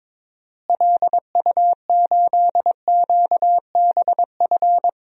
Morse code, heard as a keyed tone, about 22 wpm, 700 Hz. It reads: LU8QBF
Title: DXCC-100 Call Signs - 6 Characters - Repeated 2x - 22wpm